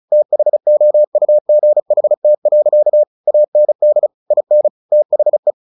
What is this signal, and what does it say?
Morse code, heard as a keyed tone, about 34 wpm, 600 Hz. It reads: THOUGHT. AND IN THE
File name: the old man and the sea / the old man and the sea 1 35wpm